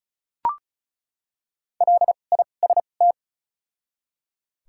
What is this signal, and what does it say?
Morse code, heard as a keyed tone, about 34 wpm, 700 Hz. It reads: L I S T
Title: 2x Character Spacing - Top 500 Words - 35wpm